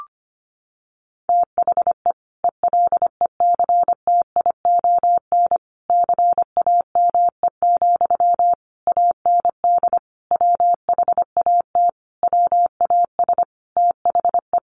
Morse code, heard as a keyed tone, about 25 wpm, 700 Hz. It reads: THE ELECTION CAME, AND WHAT WAS THE
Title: Sentences from Top 1500 Words 25wpm